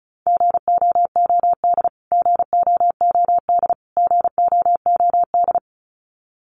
Morse code, heard as a keyed tone, about 34 wpm, 700 Hz. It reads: GOOD GOOD GOOD
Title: Top 300 Words Repeated 35wpm